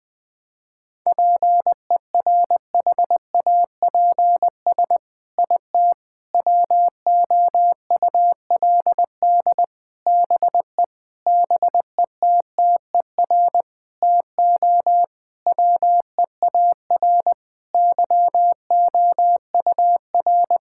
Morse code, heard as a keyed tone, about 20 wpm, 700 Hz. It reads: PERHAPS IT WOULD BE BETTER TO WEAR YOUR
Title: Sentences from Top 1600 Words 20wpm